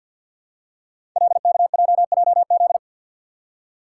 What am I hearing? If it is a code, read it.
Morse code, heard as a keyed tone, about 53 wpm, 700 Hz. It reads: LKJJZ
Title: Code Groups - 5 Letters - 50wpm